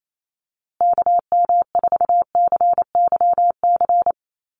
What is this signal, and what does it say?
Morse code, heard as a keyed tone, about 28 wpm, 690 Hz. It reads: KM4CYC
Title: USA Call Signs Repeated 3x 28wpm